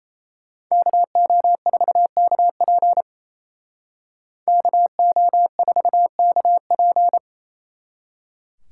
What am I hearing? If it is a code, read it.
Morse code, heard as a keyed tone, about 36 wpm, 700 Hz. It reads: KO4KP KO4KP